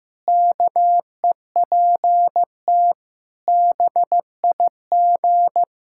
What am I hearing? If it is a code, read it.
Morse code, heard as a keyed tone, about 15 wpm, 700 Hz. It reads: KEPT BIG
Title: Pangram Challenge v2 - 15wpm